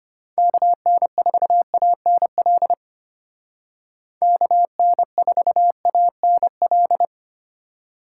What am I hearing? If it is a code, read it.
Morse code, heard as a keyed tone, about 30 wpm, 700 Hz. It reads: KN4ANL KN4ANL
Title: Speed Racing - USA 2x3 Call Signs 25wpm